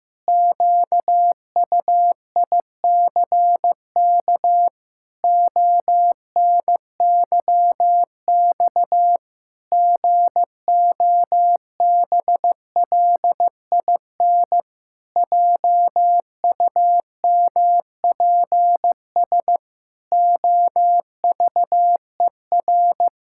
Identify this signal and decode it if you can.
Morse code, heard as a keyed tone, about 15 wpm, 700 Hz. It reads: QUICK ONYX GOBLIN JUMPS OVER
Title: Pangram Challenge v2 - 15wpm